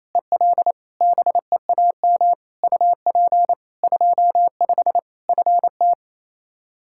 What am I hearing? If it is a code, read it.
Morse code, heard as a keyed tone, about 28 wpm, 700 Hz. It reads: EL BEAM UP 25 FT
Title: Practice QSO - Exchange 2 - Info - 28wpm